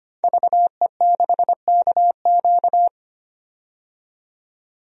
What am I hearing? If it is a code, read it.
Morse code, heard as a keyed tone, about 25 wpm, 700 Hz. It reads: VE6KQ
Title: DXCC-100 Call Signs - 5 Characters - 25wpm